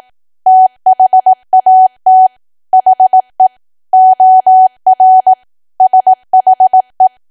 Morse code, heard as a keyed tone, about 18 wpm, 750 Hz. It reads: THAT HE OR SHE
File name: cwa / W1AW-Level2 / 18WPM